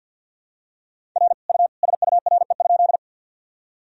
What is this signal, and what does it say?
Morse code, heard as a keyed tone, about 48 wpm, 700 Hz. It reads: R U SURE?
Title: Sentences from Top 300 Words and Abbreviations 50wpm